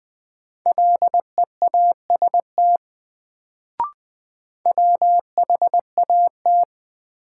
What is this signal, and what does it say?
Morse code, heard as a keyed tone, about 20 wpm, 700 Hz. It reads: LEAST WHAT
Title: Sets of 2 Words - New Word Drill - 20wpm